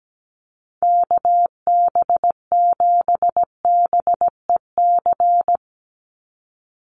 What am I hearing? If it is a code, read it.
Morse code, heard as a keyed tone, about 17 wpm, 700 Hz. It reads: KB7BEC
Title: USA Call Signs 17wpm